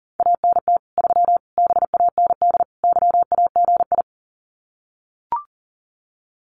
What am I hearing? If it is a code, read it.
Morse code, heard as a keyed tone, about 40 wpm, 700 Hz. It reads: ANT 3 BAND YAGI
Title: Common QSO Phrases 40wpm